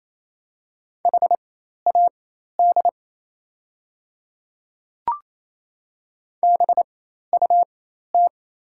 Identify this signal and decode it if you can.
Morse code, heard as a keyed tone, about 28 wpm, spaced as for 13 wpm, 690 Hz. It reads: HAD BUT
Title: 4x Character Spacing - Top 100 Words - 28wpm